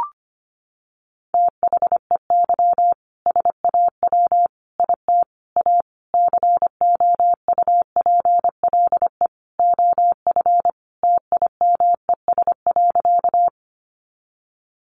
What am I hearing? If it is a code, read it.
Morse code, heard as a keyed tone, about 25 wpm, 700 Hz. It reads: THEY SAW IT A COUPLE OF TIMES.
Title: Sentences from Top 1100 Words 25wpm